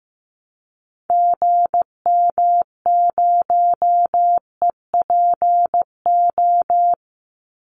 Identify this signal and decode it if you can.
Morse code, heard as a keyed tone, about 15 wpm, 700 Hz. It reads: GM0EPO
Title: Call Signs Repeated 2x - 15wpm